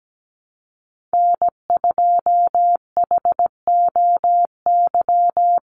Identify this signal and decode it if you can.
Morse code, heard as a keyed tone, about 17 wpm, 700 Hz. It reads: N2HOY
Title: USA Call Signs 17wpm